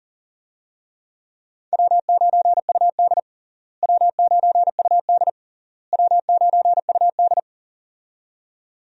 Morse code, heard as a keyed tone, about 37 wpm, 700 Hz. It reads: W9UD W9UD W9UD
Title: USA Call Signs - 4 Characters - Repeated 3x - 40wpm